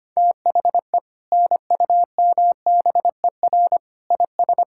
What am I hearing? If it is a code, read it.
Morse code, heard as a keyed tone, about 25 wpm, 700 Hz. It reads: THE NUMBER IS
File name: practice / 25wpm / Sentences with 2-Word Phrases - 25wpm